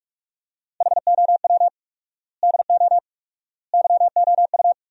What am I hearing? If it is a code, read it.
Morse code, heard as a keyed tone, about 41 wpm, 700 Hz. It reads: HOW DO YOU
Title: 4x Word-Spacing - Sentences from Top 400 Words - 45wpm